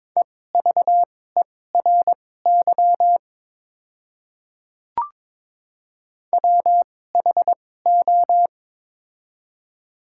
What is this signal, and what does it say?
Morse code, heard as a keyed tone, about 22 wpm, spaced as for 16 wpm, 700 Hz. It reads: EVERY WHO
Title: 2x Character Spacing - Top 200 Words - 22wpm